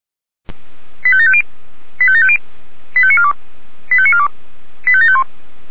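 Five-tone selective call (ZVEI), heard as quick runs of five tones, tones of about 70 ms. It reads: 85750 85750 85942 85942 85751
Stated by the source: Five-tone ZVEI selective call received by SDR, from the Feuerwehr Orlamünde (f85360) audio archive.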